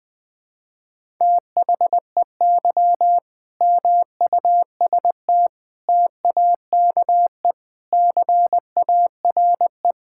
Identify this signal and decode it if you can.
Morse code, heard as a keyed tone, about 20 wpm, 700 Hz. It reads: THEY MUST TAKE CARE
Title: Sentences from Top 1000 Words - Review Part 6 - 20wpm